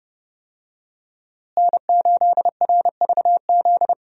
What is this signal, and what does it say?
Morse code, heard as a keyed tone, about 32 wpm, 700 Hz. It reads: N8RVZ
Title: USA Call Signs - 5 Characters - Repeated 2x - 30wpm